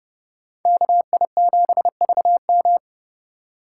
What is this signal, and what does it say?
Morse code, heard as a keyed tone, about 38 wpm, 700 Hz. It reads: KI7VM